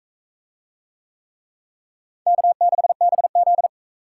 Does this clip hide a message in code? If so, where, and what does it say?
Yes: Morse code from 2.3 to 3.7 s, heard as a keyed tone, about 40 wpm, 700 Hz. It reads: K6BZ